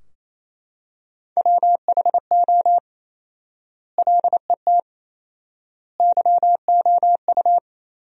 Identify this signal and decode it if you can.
Morse code, heard as a keyed tone, about 28 wpm, 700 Hz. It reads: WHO LET YOU
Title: 4x Word-Spacing - Sentences from Top 300 Words - 28wpm